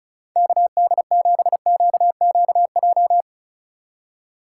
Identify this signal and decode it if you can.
Morse code, heard as a keyed tone, about 45 wpm, 700 Hz. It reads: KD7QQJ